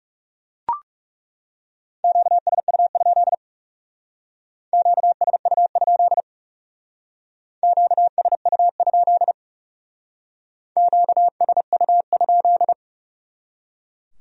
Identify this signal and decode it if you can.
Morse code, heard as a keyed tone, about 40 wpm, 700 Hz. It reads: QSU? QSU? QSU? QSU?